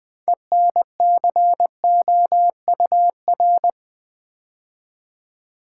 Morse code, heard as a keyed tone, about 20 wpm, 700 Hz. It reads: ENCOUR
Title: Build A Word - 3 Letters At A Time - 9-Letter Words - 20wpm